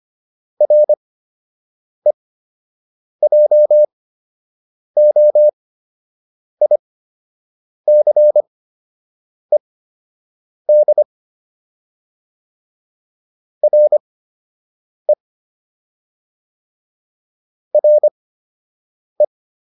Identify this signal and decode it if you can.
Morse code, heard as a keyed tone, about 25 wpm, spaced as for 7 wpm, 600 Hz. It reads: REJOICED RE RE
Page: fundamental-practice-files